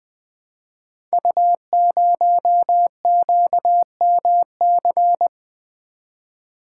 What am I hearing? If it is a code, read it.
Morse code, heard as a keyed tone, about 20 wpm, 700 Hz. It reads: U0QMC
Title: Code Groups - 5 Letter:Numbers - 20wpm